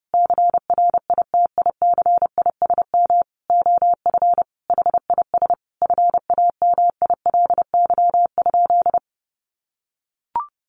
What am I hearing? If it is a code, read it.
Morse code, heard as a keyed tone, about 30 wpm, 700 Hz. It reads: CRITICISM OF HIS FAMILY?
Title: Sentences from Top 1300 Words 30wpm